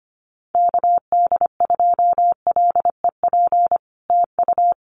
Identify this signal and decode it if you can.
Morse code, heard as a keyed tone, about 25 wpm, 700 Hz. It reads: KD2LEP TU
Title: Common QSO Phrases Encore 25wpm